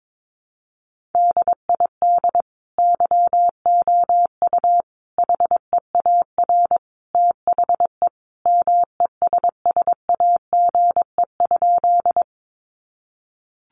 Morse code, heard as a keyed tone, about 22 wpm, 700 Hz. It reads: DID YOU HEAR THE MESSAGE?
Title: Sentences from Top 1100 Words 22wpm